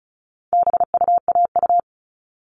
Morse code, heard as a keyed tone, about 34 wpm, 700 Hz. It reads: BUAU
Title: All Cut Numbers - 4-Digit 35wpm